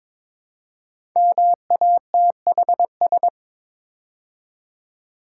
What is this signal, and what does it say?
Morse code, heard as a keyed tone, about 21 wpm, 700 Hz. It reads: MATHS
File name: 5 Letter Words - First 3 Letters Given - 22wpm